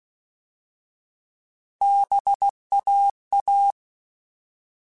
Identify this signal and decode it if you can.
Morse code, heard as a keyed tone, about 16 wpm, 800 Hz. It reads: BAA
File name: Alphabetical / Alphabetical mp3s / Bs @15